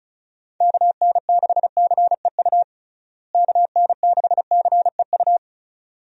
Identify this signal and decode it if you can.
Morse code, heard as a keyed tone, about 36 wpm, 700 Hz. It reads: KN6CEU KN6CEU
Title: USA Call Signs - 6 Characters - Repeated 2x - 35wpm